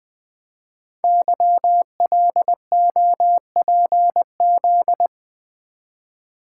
Morse code, heard as a keyed tone, about 20 wpm, 700 Hz. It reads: YLOPZ
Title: Code Groups - 5 Letters - 20wpm